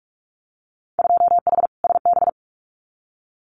Morse code, heard as a keyed tone, about 46 wpm, 700 Hz. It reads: 2H SB
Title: Winter Field Day Exchanges Encore 45wpm